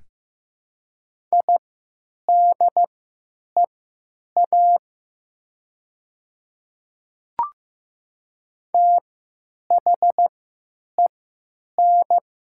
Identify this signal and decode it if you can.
Morse code, heard as a keyed tone, about 15 wpm, spaced as for 8 wpm, 700 Hz. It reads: IDEA THEN
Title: 3x Character Spacing - Top 300 Words - 15wpm